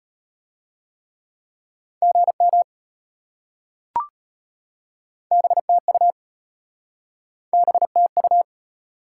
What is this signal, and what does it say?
Morse code, heard as a keyed tone, about 33 wpm, 690 Hz. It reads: GM BTU BTU